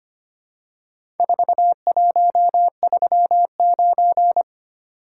Morse code, heard as a keyed tone, about 25 wpm, 700 Hz. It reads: 4139
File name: practice / 25wpm / Four Letter-Number Combinations - 25wpm